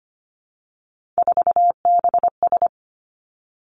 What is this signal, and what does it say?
Morse code, heard as a keyed tone, about 25 wpm, 700 Hz. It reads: 4BS
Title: Three Challenging Letter-Number Combinations 25wpm